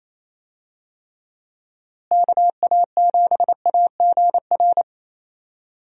Morse code, heard as a keyed tone, about 28 wpm, 700 Hz. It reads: KA7AGR
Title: USA Call Signs - 6 Characters - Repeated 2x - 28wpm